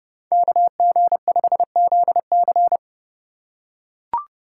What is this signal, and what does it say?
Morse code, heard as a keyed tone, about 30 wpm, 700 Hz. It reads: KG5ZC